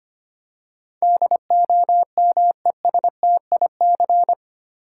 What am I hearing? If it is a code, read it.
Morse code, heard as a keyed tone, about 25 wpm, 700 Hz. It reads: DOMESTIC
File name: Eight Letter Words - 25wpm